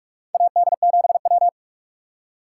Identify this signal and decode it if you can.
Morse code, heard as a keyed tone, about 44 wpm, 700 Hz. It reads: AD7W
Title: USA Call Signs - 6x Character Spacing - 45wpm